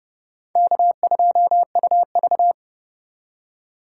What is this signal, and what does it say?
Morse code, heard as a keyed tone, about 37 wpm, 700 Hz. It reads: K2UV
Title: Speed Racing - USA Call Signs 30wpm